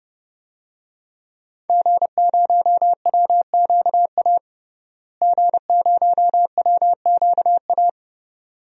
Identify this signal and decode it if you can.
Morse code, heard as a keyed tone, about 30 wpm, 700 Hz. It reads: G0WQA G0WQA
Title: DXCC-100 Call Signs - 5 Characters - Repeated 2x - 30wpm